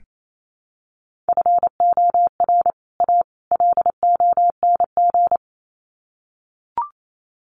Morse code, heard as a keyed tone, about 28 wpm, 700 Hz. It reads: FOR A LONG
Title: Sets of 3 Words 28wpm